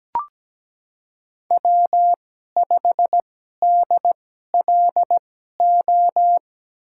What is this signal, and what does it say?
Morse code, heard as a keyed tone, about 17 wpm, spaced as for 12 wpm, 700 Hz. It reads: W5DLO
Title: USA Call Signs - 5 Characters - 2x Character Spacing - 17wpm